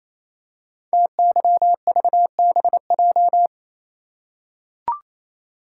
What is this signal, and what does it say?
Morse code, heard as a keyed tone, about 28 wpm, 700 Hz. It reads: TYVBJ
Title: Code Groups - 5 Letters - 28wpm